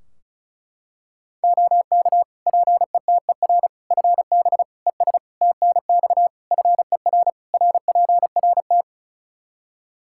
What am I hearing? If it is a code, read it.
Morse code, heard as a keyed tone, about 34 wpm, 700 Hz. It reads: OK PETER FB ES TNX FER RPRT
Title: K1USN SST QSO Practice 35wpm